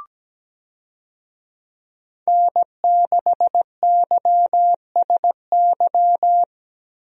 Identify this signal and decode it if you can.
Morse code, heard as a keyed tone, about 17 wpm, 700 Hz. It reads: N6YSY
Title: USA Call Signs - 5 Characters - Repeated 2x - 17wpm